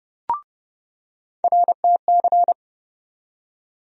Morse code, heard as a keyed tone, about 31 wpm, 700 Hz. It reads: RTC
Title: USA Call Sign Postfixes - 3 Letters 30wpm